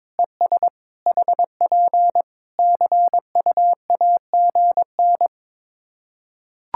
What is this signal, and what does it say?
Morse code, heard as a keyed tone, about 21 wpm, 700 Hz. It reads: ES HP CUAGN
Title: K1USN SST QSO Practice 22wpm